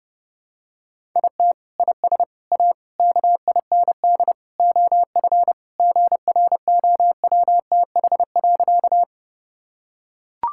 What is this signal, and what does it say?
Morse code, heard as a keyed tone, about 30 wpm, 700 Hz. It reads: IT IS A KIND OF GROWTH.
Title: Sentences from Top 1000 Words - Review Part 5 - 30wpm